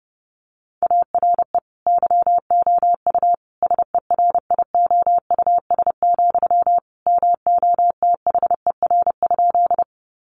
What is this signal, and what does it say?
Morse code, heard as a keyed tone, about 30 wpm, 700 Hz. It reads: ARE YOU SERIOUS, MOTHER?
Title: Sentences from Top 1100 Words 30wpm